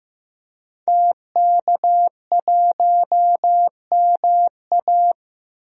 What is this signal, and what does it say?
Morse code, heard as a keyed tone, about 15 wpm, 700 Hz. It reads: TK1MA
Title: DXCC-100 Call Signs - 5 Characters - 15wpm